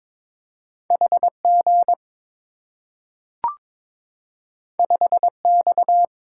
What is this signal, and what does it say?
Morse code, heard as a keyed tone, about 23 wpm, 690 Hz. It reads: HG 5X
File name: practice / 22wpm / Two Letter-Number Combinations - 22wpm